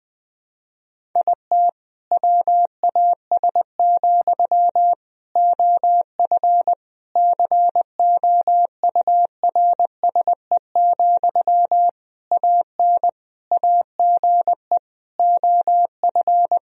Morse code, heard as a keyed tone, about 20 wpm, 700 Hz. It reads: IT WAS, OF COURSE, AN AGE OF
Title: Sentences from Top 1000 Words - Review Part 3 - 20wpm